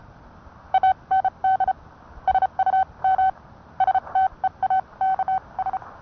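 Morse code, heard as a keyed tone, about 30 wpm, spaced as for 25 wpm, 760 Hz. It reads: AND SUM STEAKS
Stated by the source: Morse code sent by N1EA on a Vibroplex key.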